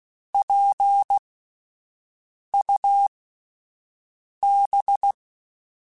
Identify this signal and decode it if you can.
Morse code, heard as a keyed tone, about 16 wpm, spaced as for 5 wpm, 800 Hz. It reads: PUB